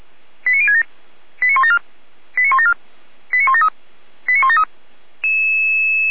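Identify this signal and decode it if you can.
Five-tone selective call (ZVEI), heard as quick runs of five tones, tones of about 70 ms. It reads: 89067 89174 89174 89173 89173
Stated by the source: ZVEI five-tone call received by SDR.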